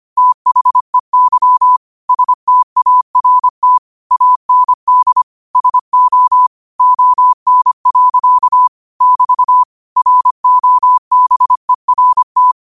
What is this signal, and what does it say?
Morse code, heard as a keyed tone, about 25 wpm, 1000 Hz. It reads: THEY START AND SO ON. <BT> ROBERT